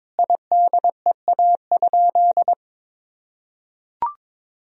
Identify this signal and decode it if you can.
Morse code, heard as a keyed tone, about 22 wpm, 700 Hz. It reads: IDEA?
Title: Sentences from Top 400 Words and Abbreviations 22wpm